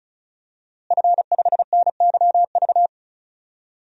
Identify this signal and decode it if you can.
Morse code, heard as a keyed tone, about 36 wpm, 700 Hz. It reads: F5NYV
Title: DXCC-100 Call Signs - 5 Characters - Repeated 3x - 35wpm